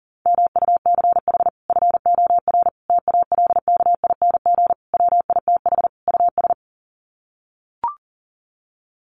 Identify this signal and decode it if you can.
Morse code, heard as a keyed tone, about 40 wpm, 700 Hz. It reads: MUCH FOR TALKING WITH US
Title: Sets of 5 Words Encore 40wpm